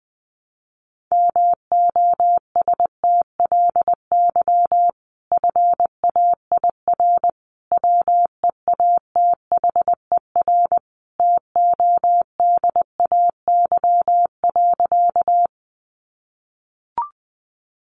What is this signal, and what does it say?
Morse code, heard as a keyed tone, about 20 wpm, 700 Hz. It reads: MOSTLY FAIR WEATHER TODAY.
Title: Sentences from Top 1300 Words 20wpm